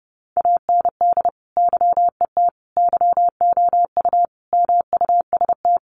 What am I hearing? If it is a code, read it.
Morse code, heard as a keyed tone, about 30 wpm, 700 Hz. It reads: AND YET YOU MUST
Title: Sentences from Top 1300 Words 30wpm